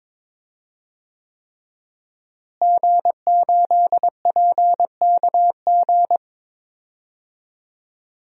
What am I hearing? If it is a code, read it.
Morse code, heard as a keyed tone, about 22 wpm, 700 Hz. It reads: G8PKG
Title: DXCC-100 Call Signs - 5 Characters - Repeated 3x - 22wpm